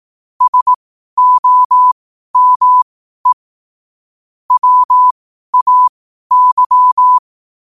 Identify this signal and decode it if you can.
Morse code, heard as a keyed tone, about 18 wpm, spaced as for 12 wpm, 1000 Hz. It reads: SOME WAY
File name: qotdfiles / 12wpm